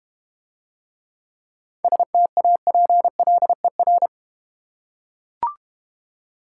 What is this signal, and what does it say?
Morse code, heard as a keyed tone, about 31 wpm, 690 Hz. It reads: STAPLER
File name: Easy Words - Speed-Racing with Hints - 25wpm